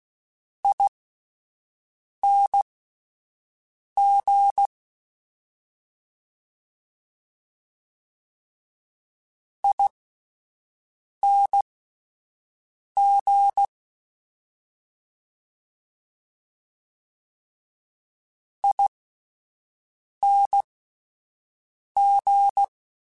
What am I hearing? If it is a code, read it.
Morse code, heard as a keyed tone, about 16 wpm, spaced as for 5 wpm, 800 Hz. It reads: ING ING ING
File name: General / General-mp3s / Suffixes sp @10